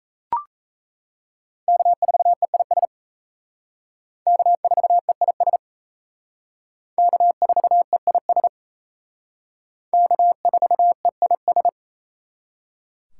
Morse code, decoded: K4EIS K4EIS K4EIS K4EIS

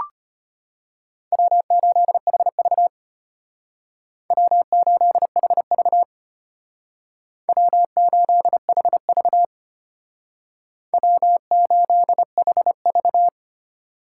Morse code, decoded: W8HV W8HV W8HV W8HV

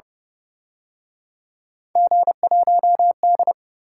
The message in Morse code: G1D

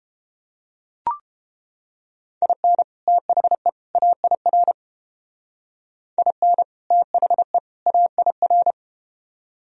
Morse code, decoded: IN THE AIR IN THE AIR

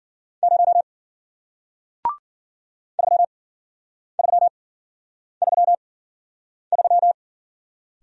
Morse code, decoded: 0 3333

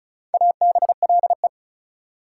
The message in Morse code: ABLE

